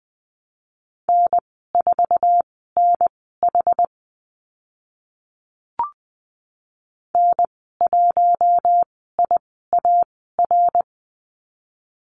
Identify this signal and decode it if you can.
Morse code, heard as a keyed tone, about 20 wpm, spaced as for 14 wpm, 700 Hz. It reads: N4NH N1IAR
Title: 076 Lesson 13g - Letter L - Call Signs 20wpm